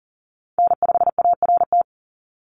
Morse code, heard as a keyed tone, about 40 wpm, 690 Hz. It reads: N5ART
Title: USA Call Signs 40wpm